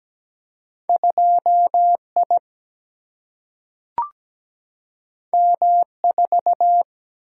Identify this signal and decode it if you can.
Morse code, heard as a keyed tone, about 17 wpm, 700 Hz. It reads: 2I M4
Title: Single-Letter Single-Number 17wpm